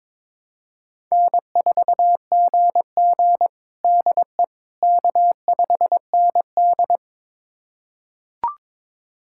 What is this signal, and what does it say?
Morse code, heard as a keyed tone, about 22 wpm, 700 Hz. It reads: N4GG DE K5ND
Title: USA Callsign DE Callsign - 4 Characters - 22wpm